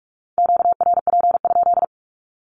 Morse code, heard as a keyed tone, about 44 wpm, 700 Hz. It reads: QRP?
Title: Q-Codes 45wpm